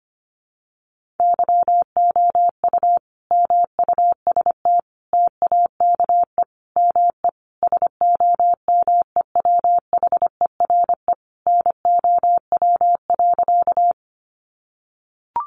Morse code, heard as a keyed tone, about 25 wpm, 700 Hz. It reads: YOU MUST TAKE ME SOMEWHERE NOW.
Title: Sentences from Top 1400 Words 25wpm